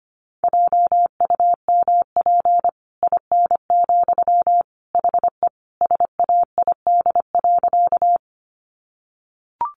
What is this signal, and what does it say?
Morse code, heard as a keyed tone, about 25 wpm, 700 Hz. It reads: JUMP IN, HE SAID.
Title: Sentences from Top 1300 Words 25wpm